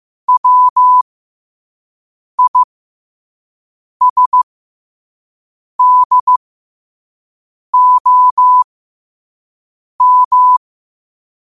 Morse code, heard as a keyed tone, about 15 wpm, spaced as for 5 wpm, 1000 Hz. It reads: WISDOM
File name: qotdfiles / 5wpm